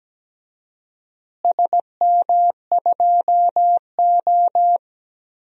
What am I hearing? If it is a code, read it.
Morse code, heard as a keyed tone, about 17 wpm, 700 Hz. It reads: SM2O
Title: DXCC-100 Call Signs - 4 Characters - Repeated 3x - 17wpm